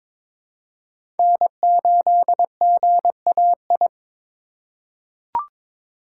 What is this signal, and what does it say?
Morse code, heard as a keyed tone, about 22 wpm, 700 Hz. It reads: N8GAI
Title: USA Call Signs - 5 Characters - 3x Character Spacing - 22wpm